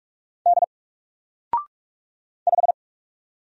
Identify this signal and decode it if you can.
Morse code, heard as a keyed tone, about 44 wpm, 700 Hz. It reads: D 5
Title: Single Letter-Number 45wpm